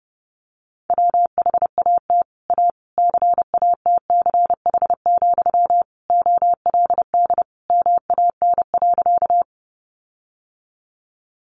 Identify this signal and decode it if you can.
Morse code, heard as a keyed tone, about 30 wpm, 700 Hz. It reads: WHAT A CATCH, OLD MAN.
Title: Sentences from Top 1500 Words 30wpm